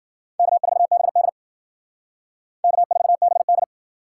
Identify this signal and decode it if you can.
Morse code, heard as a keyed tone, about 53 wpm, 700 Hz. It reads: K4BD K4BD